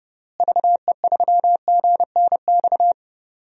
Morse code, heard as a keyed tone, about 30 wpm, 700 Hz. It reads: VE3GNX
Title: DXCC-100 Call Signs - 6 Characters - Repeated 2x - 30wpm